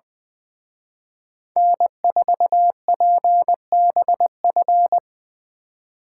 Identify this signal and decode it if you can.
Morse code, heard as a keyed tone, about 24 wpm, 700 Hz. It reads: N4PBF
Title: Speed Racing - USA 1x3 Call Signs 20wpm